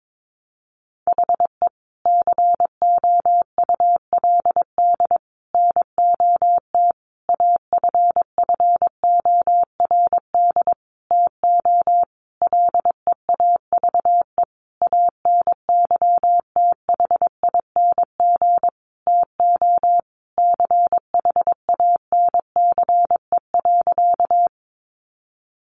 Morse code, heard as a keyed tone, about 22 wpm, 700 Hz. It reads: HE COULD NOT AFFORD TO LEAVE ANYTHING TO CHANCE.
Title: Sentences from Top 1100 Words 22wpm